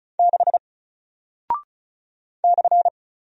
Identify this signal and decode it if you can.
Morse code, heard as a keyed tone, about 36 wpm, 700 Hz. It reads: 6 /
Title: Single Character (Letter:Number:Punctuation:Prosign) 35wpm